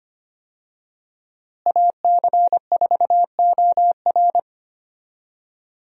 Morse code, heard as a keyed tone, about 25 wpm, 700 Hz. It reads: AC4OR
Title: USA Call Signs - 5 Characters - Repeated 3x - 25wpm